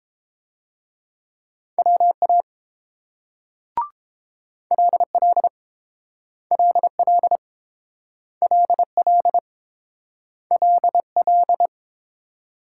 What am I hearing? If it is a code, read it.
Morse code, heard as a keyed tone, about 30 wpm, 700 Hz. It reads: WA LL LL LL LL